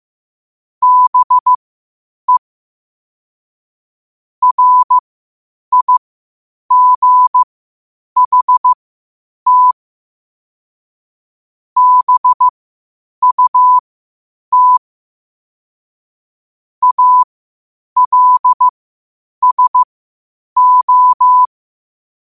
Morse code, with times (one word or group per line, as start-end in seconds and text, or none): 0.8-2.4 BE
4.4-9.7 RIGHT
11.8-14.8 BUT
16.8-21.4 ALSO